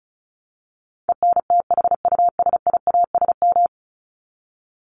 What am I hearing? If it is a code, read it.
Morse code, heard as a keyed tone, about 34 wpm, 700 Hz. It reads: ENTHUSIASM
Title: Top 1751-2000 Words 35wpm